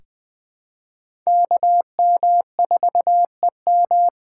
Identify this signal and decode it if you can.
Morse code, heard as a keyed tone, about 20 wpm, 700 Hz. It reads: KM4EM
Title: USA Call Signs - 5 Characters - 2x Character Spacing - 20wpm